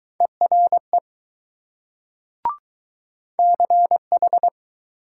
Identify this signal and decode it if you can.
Morse code, heard as a keyed tone, about 20 wpm, 700 Hz. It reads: ERE CH